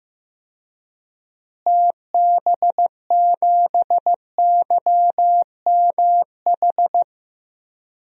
Morse code, heard as a keyed tone, about 15 wpm, 700 Hz. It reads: TB7YMH
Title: DXCC-100 Call Signs - 6 Characters - Repeated 3x - 15wpm